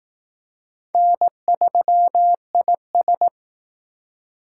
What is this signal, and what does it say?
Morse code, heard as a keyed tone, about 18 wpm, 700 Hz. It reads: N3IS
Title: Speed Racing - USA 1x2 Call Signs 15wpm